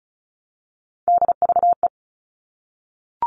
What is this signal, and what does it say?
Morse code, heard as a keyed tone, about 34 wpm, 700 Hz. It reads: DVE
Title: All Cut Numbers - 3-Digit 35wpm